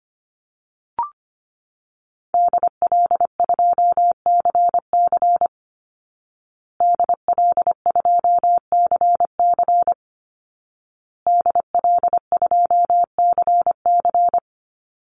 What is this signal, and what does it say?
Morse code, heard as a keyed tone, about 25 wpm, 700 Hz. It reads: DL2CC DL2CC DL2CC
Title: Call Signs Repeated 2x - 25wpm